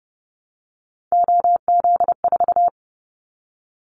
Morse code, heard as a keyed tone, about 28 wpm, 700 Hz. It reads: OZ4
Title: Callsign Prefixes for Top 100 DXCC - 30wpm